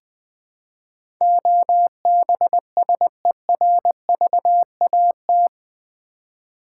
Morse code, heard as a keyed tone, about 20 wpm, 700 Hz. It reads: OBSERVAT
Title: Build A Word - 4 Letters At A Time - 11-Letter Words - 20wpm